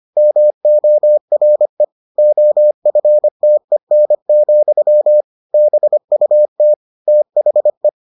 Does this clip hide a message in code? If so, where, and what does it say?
Yes: Morse code from 0.2 to 7.9 s, heard as a keyed tone, about 25 wpm, 600 Hz. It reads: MORE OFTEN, BUT THE